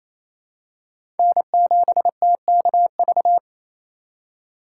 Morse code, heard as a keyed tone, about 28 wpm, 700 Hz. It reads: N7TKV
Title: USA Call Signs - 5 Characters - 2x Character Spacing - 28wpm